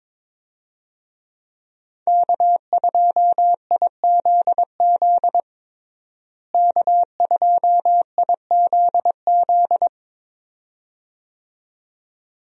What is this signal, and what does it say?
Morse code, heard as a keyed tone, about 22 wpm, 700 Hz. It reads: K2IZZ K2IZZ
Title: USA Call Signs - 5 Characters - Repeated 2x - 22wpm